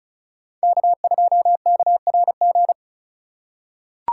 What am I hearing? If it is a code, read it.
Morse code, heard as a keyed tone, about 36 wpm, 700 Hz. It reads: K2KRG
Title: USA Call Signs - 5 Characters - 2x Character Spacing - 35wpm